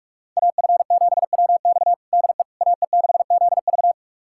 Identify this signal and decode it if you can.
Morse code, heard as a keyed tone, about 41 wpm, 700 Hz. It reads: AF7WX DE AE6ZV
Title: USA Callsign DE Callsign - 5 Characters - 45wpm